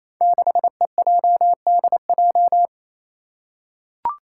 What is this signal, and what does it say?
Morse code, heard as a keyed tone, about 28 wpm, 700 Hz. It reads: 6EJDJ
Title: Five Challenging Letter-Number Combinations 28wpm